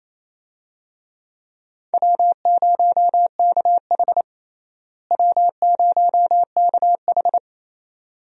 Morse code, heard as a keyed tone, about 28 wpm, 700 Hz. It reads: W0KH W0KH
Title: USA Call Signs - 4 Characters - Repeated 2x - 28wpm